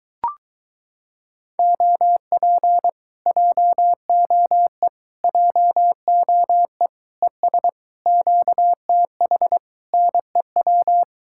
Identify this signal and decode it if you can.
Morse code, heard as a keyed tone, about 24 wpm, 700 Hz. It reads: OP JOE JOE ES QTH NEW